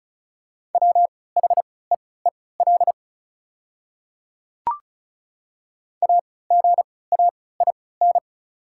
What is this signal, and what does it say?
Morse code, heard as a keyed tone, about 34 wpm, spaced as for 20 wpm, 690 Hz. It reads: WHEEL AGAIN
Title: 3x Character Spacing - Top 500 Words - 35wpm